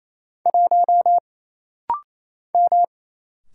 Morse code, heard as a keyed tone, about 27 wpm, 700 Hz. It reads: 1 M
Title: Single Letter-Number - Rapid-Fire 28wpm